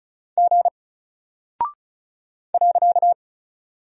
Morse code, heard as a keyed tone, about 36 wpm, 700 Hz. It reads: G .
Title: Single Character (Letter:Number:Punctuation:Prosign) Alt 35wpm